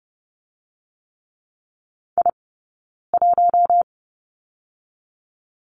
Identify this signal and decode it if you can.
Morse code, heard as a keyed tone, about 30 wpm, 700 Hz. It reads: I 1
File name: Lesson 16d - Letter c - Two Characters 30wpm